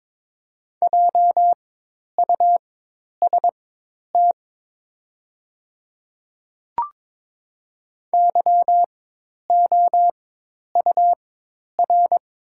Morse code, heard as a keyed tone, about 22 wpm, spaced as for 10 wpm, 700 Hz. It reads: JUST YOUR
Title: 4x Character Spacing - Top 300 Words - 22wpm